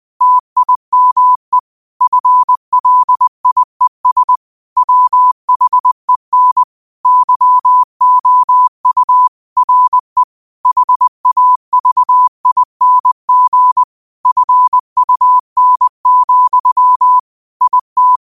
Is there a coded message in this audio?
Yes: Morse code from 0.2 to 18.2 s, heard as a keyed tone, about 20 wpm, 1000 Hz. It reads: TIME FLIES WHEN YOU RE HAVING FUN, IT